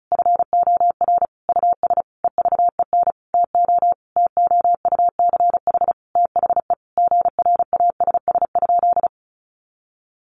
Morse code, heard as a keyed tone, about 34 wpm, 700 Hz. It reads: FOR US EVEN TO TOUCH THE GRASS?
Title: A Sound Of Thunder 35wpm